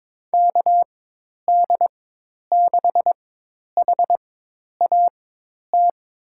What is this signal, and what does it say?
Morse code, heard as a keyed tone, about 22 wpm, spaced as for 10 wpm, 700 Hz. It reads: KD6HAT
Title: USA Call Signs - 4x Character Spacing - 22wpm